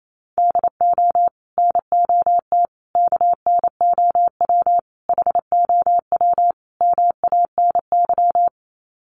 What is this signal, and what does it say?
Morse code, heard as a keyed tone, about 28 wpm, 700 Hz. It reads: DO NOT KNOW HOW MANY
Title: Sets of 5 Words Encore 28wpm